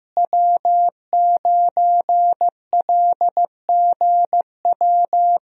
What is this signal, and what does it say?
Morse code, heard as a keyed tone, about 15 wpm, 700 Hz. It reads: W9LGW
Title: USA Call Signs - 5 Characters - Repeated 2x - 15wpm